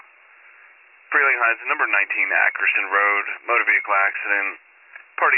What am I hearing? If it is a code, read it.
Emergency Alert System (SAME) tones (header bursts), heard as harsh buzzing bursts, with speech. It is heard from 0.2 to 4.5 s.